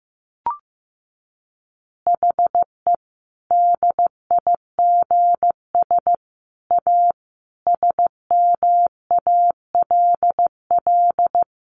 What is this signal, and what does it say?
Morse code, heard as a keyed tone, about 15 wpm, 700 Hz. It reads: HE DIGS A SMALL
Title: Easy Phrases 15wpm